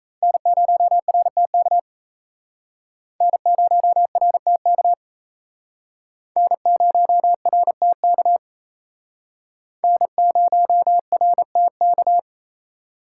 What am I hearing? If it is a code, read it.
Morse code, heard as a keyed tone, about 35 wpm, 700 Hz. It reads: N0RTK N0RTK N0RTK N0RTK